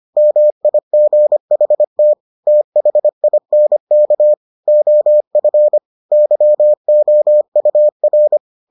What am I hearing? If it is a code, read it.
Morse code, heard as a keyed tone, about 25 wpm, 600 Hz. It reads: MIGHT THINK OF YOUR